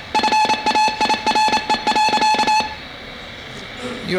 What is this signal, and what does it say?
Morse code, heard as a keyed tone, about 28 wpm, 860 Hz, with speech. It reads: FAIRE.